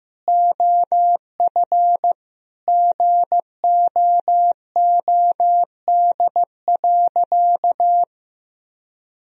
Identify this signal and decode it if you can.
Morse code, heard as a keyed tone, about 15 wpm, 700 Hz. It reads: OF GOOD.
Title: Sentences from Top 1000 Words - Review Part 3 - 15wpm